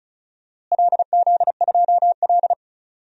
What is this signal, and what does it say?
Morse code, heard as a keyed tone, about 36 wpm, 700 Hz. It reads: LZ2L
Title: DXCC-100 Call Signs - 4 Characters - Repeated 3x - 35wpm